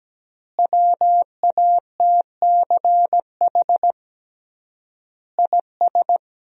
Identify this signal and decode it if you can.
Morse code, heard as a keyed tone, about 17 wpm, 700 Hz. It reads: WATCH IS
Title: 3x Word-Spacing - Sentences from Top 400 Words - 17wpm